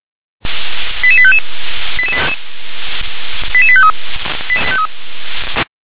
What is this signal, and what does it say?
Five-tone selective call (ZVEI), heard as quick runs of five tones, tones of about 70 ms. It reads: 88855 88853